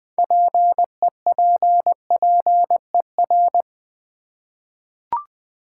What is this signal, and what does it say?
Morse code, heard as a keyed tone, about 20 wpm, 700 Hz. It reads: PEPPER
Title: 6 Letter Words - First 3 Letters Given - 20wpm